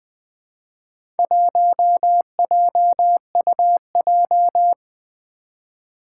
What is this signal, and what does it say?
Morse code, heard as a keyed tone, about 20 wpm, 700 Hz. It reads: 1JUJ
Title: Four Challenging Letter-Number Combinations 20wpm